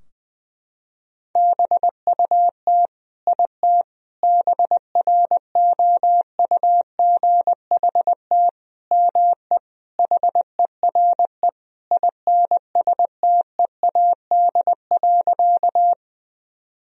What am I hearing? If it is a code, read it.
Morse code, heard as a keyed tone, about 21 wpm, 700 Hz. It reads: BUT IT BROUGHT ME HERE INSTEAD.
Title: Sentences from Top 1000 Words - Review Part 4 - 20wpm